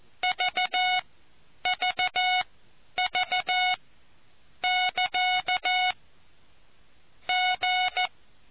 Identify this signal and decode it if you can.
Morse code, heard as a keyed tone, about 14 wpm, spaced as for 9 wpm, 730 Hz. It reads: VVV <KA> G